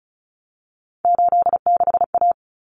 Morse code, heard as a keyed tone, about 34 wpm, 700 Hz. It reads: 86A
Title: Common Cut Numbers - 3-Digit 35wpm